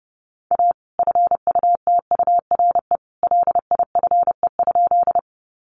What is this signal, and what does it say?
Morse code, heard as a keyed tone, about 30 wpm, 700 Hz. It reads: A FUTURE LIFE?
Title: Sentences from Top 700 Words 30wpm